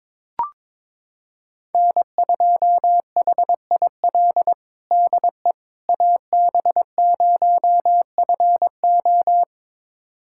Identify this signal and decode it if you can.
Morse code, heard as a keyed tone, about 22 wpm, 700 Hz. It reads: N2HIL DE AB0FO
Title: USA Callsign DE Callsign - 5 Characters - 22wpm